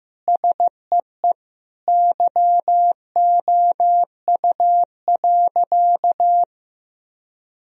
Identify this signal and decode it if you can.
Morse code, heard as a keyed tone, about 14 wpm, 700 Hz. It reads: SEE YOU.
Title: Sentences from Top 1000 Words - Review Part 3 - 15wpm